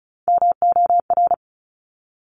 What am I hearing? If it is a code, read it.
Morse code, heard as a keyed tone, about 36 wpm, 700 Hz. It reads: MOR
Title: Top Three Letter Combinations in Words 35wpm